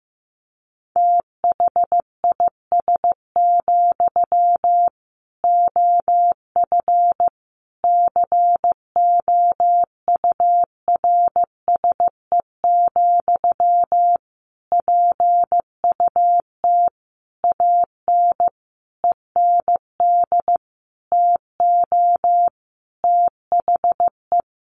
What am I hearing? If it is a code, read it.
Morse code, heard as a keyed tone, about 14 wpm, 700 Hz. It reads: THIS, OF COURSE, PUT AN END TO THE
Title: Sentences from Top 1400 Words 15wpm